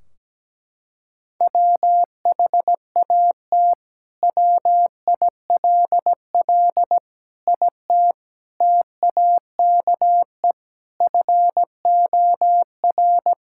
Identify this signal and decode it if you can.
Morse code, heard as a keyed tone, about 17 wpm, 700 Hz. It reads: WHAT WILL IT TAKE FOR
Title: Sentences from Top 1600 Words 17wpm